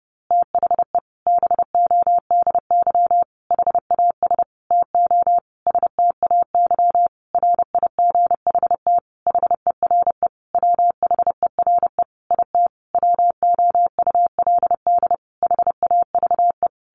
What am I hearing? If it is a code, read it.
Morse code, heard as a keyed tone, about 30 wpm, 700 Hz. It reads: THE BODY HAS TO STAY RIGHT HERE WHERE IT WOULD HAVE
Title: A Sound Of Thunder 30wpm